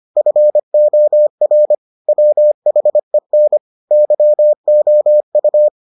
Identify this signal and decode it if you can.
Morse code, heard as a keyed tone, about 25 wpm, 600 Hz. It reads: FOR WHEN YOU